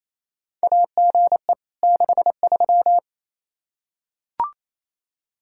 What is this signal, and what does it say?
Morse code, heard as a keyed tone, about 28 wpm, 700 Hz. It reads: AGE 63
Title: K1USN SST QSO Practice 28wpm